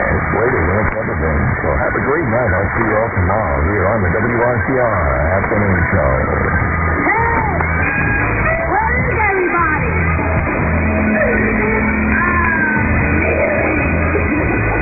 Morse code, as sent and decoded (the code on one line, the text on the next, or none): none